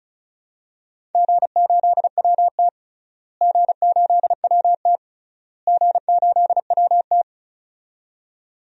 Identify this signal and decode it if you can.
Morse code, heard as a keyed tone, about 34 wpm, 700 Hz. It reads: G8WT G8WT G8WT
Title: DXCC-100 Call Signs - 4 Characters - Repeated 3x - 35wpm